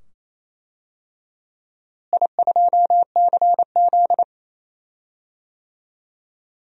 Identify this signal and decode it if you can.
Morse code, heard as a keyed tone, about 28 wpm, 700 Hz. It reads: I2CZ
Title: DXCC-100 Call Signs - 4 Characters - Repeated 2x - 28wpm